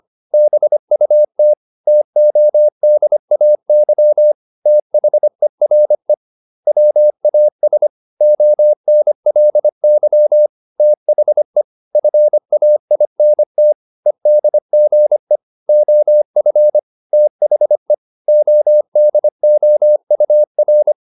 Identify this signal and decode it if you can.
Morse code, heard as a keyed tone, about 25 wpm, 600 Hz. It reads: BUT TODAY THERE WAS ONLY THE FAINT EDGE OF THE ODOUR
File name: the old man and the sea part 1 25wpm